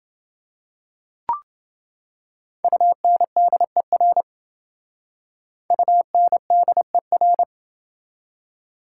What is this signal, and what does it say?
Morse code, heard as a keyed tone, about 27 wpm, 700 Hz. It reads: UNDER UNDER